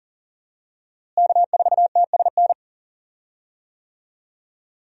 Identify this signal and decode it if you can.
Morse code, heard as a keyed tone, about 40 wpm, 700 Hz. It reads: K4TSN
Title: USA Call Signs - 5 Characters - Repeated 2x - 40wpm